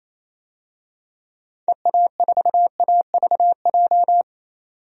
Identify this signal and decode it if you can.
Morse code, heard as a keyed tone, about 27 wpm, 700 Hz. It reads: EA4AVJ
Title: DXCC-100 Call Signs - 6 Characters - 28wpm